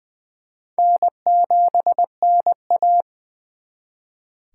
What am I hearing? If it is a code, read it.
Morse code, heard as a keyed tone, about 25 wpm, 700 Hz. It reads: N7NA